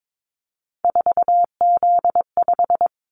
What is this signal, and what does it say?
Morse code, heard as a keyed tone, about 22 wpm, 700 Hz. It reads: 4Z5
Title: Callsign Prefixes for Top 50 DXCC - 22wpm